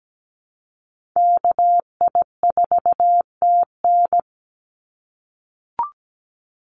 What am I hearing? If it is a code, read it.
Morse code, heard as a keyed tone, about 17 wpm, 700 Hz. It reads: KI4TN
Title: NA SOTA Activator Callsigns 17wpm